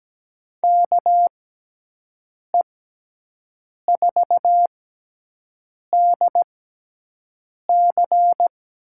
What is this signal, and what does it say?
Morse code, heard as a keyed tone, about 17 wpm, spaced as for 6 wpm, 700 Hz. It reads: KE4DC
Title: USA Call Signs - 6 Characters - 6x Character Spacing - 17wpm